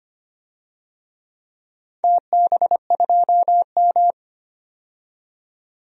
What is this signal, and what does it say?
Morse code, heard as a keyed tone, about 25 wpm, 700 Hz. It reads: TB2M
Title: DXCC-100 Call Signs - 4 Characters - 25wpm